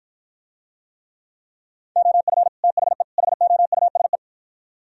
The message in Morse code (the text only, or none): OF THE HOUSE